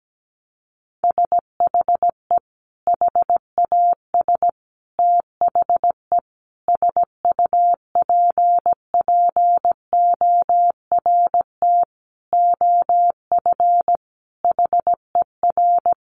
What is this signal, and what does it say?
Morse code, heard as a keyed tone, about 17 wpm, 700 Hz. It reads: SHE HAS THE SUPPORT OF HER
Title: Sentences from Top 800 Words 17wpm